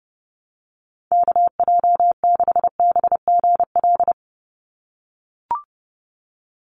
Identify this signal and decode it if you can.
Morse code, heard as a keyed tone, about 30 wpm, 700 Hz. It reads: KJ6BGL
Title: USA Call Signs 30wpm